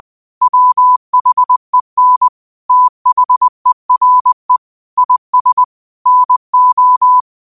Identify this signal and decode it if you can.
Morse code, heard as a keyed tone, about 20 wpm, 1000 Hz. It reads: WHEN THERE IS NO